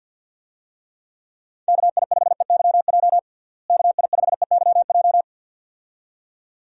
Morse code, heard as a keyed tone, about 48 wpm, 700 Hz. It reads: KI5EYJ KI5EYJ
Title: USA Call Signs - 6 Characters - Repeated 2x - 50wpm